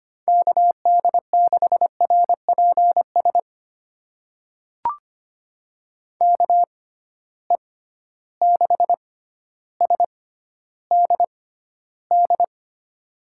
Morse code, heard as a keyed tone, about 25 wpm, spaced as for 13 wpm, 700 Hz. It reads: KD6RPS KE6SDD
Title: USA Call Signs - 6 Characters - 6x Character Spacing - 25wpm